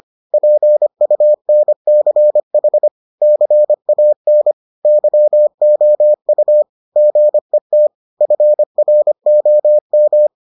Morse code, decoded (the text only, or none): PUNCH CAN YOU GET FROM